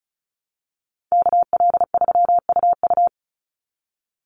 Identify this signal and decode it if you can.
Morse code, heard as a keyed tone, about 34 wpm, 700 Hz. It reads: KL3UU
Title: USA Call Signs 35wpm